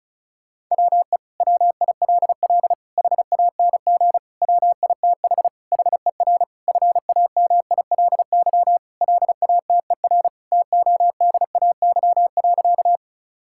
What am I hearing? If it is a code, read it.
Morse code, heard as a keyed tone, about 34 wpm, 700 Hz. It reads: WE WILL HANG WITH HER FAMILY LATER TODAY.
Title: Sentences from Top 1600 Words 35wpm